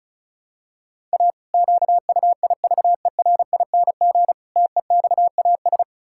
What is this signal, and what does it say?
Morse code, heard as a keyed tone, about 36 wpm, 700 Hz. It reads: A QUIVERING TEXAS
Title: Pangram Challenge v2 - 35wpm